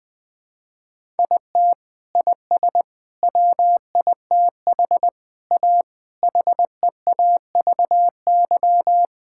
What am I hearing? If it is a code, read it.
Morse code, heard as a keyed tone, about 21 wpm, 700 Hz. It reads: IT IS WITH A HEAVY
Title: Sets of 8 Words - 20wpm